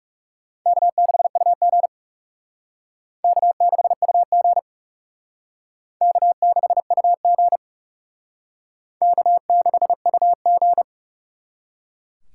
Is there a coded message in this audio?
Yes: Morse code from 0.7 to 10.8 s, heard as a keyed tone, about 38 wpm, 700 Hz. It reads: K6UG K6UG K6UG K6UG